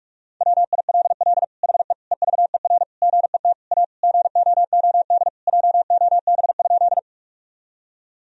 Morse code, heard as a keyed tone, about 46 wpm, 700 Hz. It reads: WILL HE EVER GET A GOOD JOB?
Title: Sentences from Top 1000 Words - Review Part 3 - 45wpm